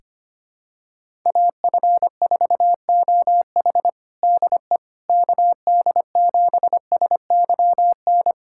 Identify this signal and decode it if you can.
Morse code, heard as a keyed tone, about 25 wpm, 700 Hz. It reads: AF4OH DE KD7SYN
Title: USA Callsign DE Callsign 25wpm